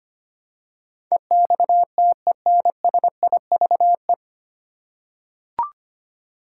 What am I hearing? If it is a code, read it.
Morse code, heard as a keyed tone, about 25 wpm, 700 Hz. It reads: EXTENSIVE
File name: Nine Letter Words - 25wpm